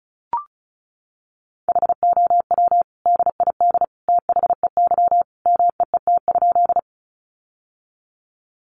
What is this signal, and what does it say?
Morse code, heard as a keyed tone, about 34 wpm, 700 Hz. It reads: HOW DID THEY MEET?
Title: Sentences from Top 1100 Words 35wpm